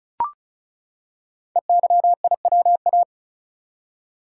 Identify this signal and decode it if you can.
Morse code, heard as a keyed tone, about 36 wpm, 700 Hz. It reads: EYIWA
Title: Code Groups - 5 Letters - 35wpm